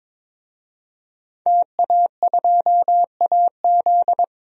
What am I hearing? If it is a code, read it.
Morse code, heard as a keyed tone, about 22 wpm, 700 Hz. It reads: TA2AZ
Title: DXCC-100 Call Signs - 5 Characters - 22wpm